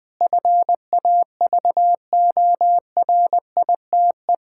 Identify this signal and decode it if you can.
Morse code, heard as a keyed tone, about 20 wpm, 700 Hz. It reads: FAVORITE
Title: Difficult Sentences with Hints - 20wpm